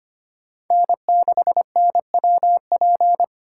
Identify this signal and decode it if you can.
Morse code, heard as a keyed tone, about 25 wpm, 700 Hz. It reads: N6NWP
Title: USA Callsign DE Callsign - 5 Characters - 25wpm